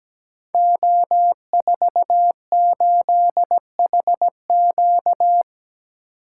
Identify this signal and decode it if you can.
Morse code, heard as a keyed tone, about 17 wpm, 700 Hz. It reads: O48HQ
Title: Code Groups - 5 Letter:Numbers - 17wpm